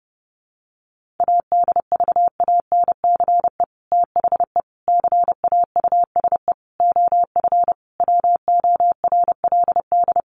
Morse code, heard as a keyed tone, about 30 wpm, 700 Hz. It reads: ADVANCE THE CAUSE OF WORLD
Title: Sentences from Top 1400 Words 30wpm